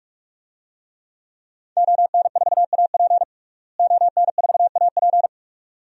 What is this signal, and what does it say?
Morse code, heard as a keyed tone, about 43 wpm, 700 Hz. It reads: ON4AP ON4AP
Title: DXCC-100 Call Signs - 5 Characters - Repeated 2x - 45wpm